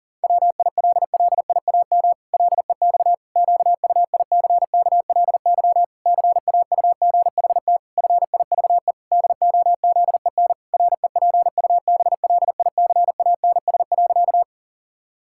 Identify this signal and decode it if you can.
Morse code, heard as a keyed tone, about 40 wpm, 700 Hz. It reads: WILLIAM LEX QUICKLY CAUGHT FIVE DOZEN REPUBLICANS.
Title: Pangram Challenge v2 - 40wpm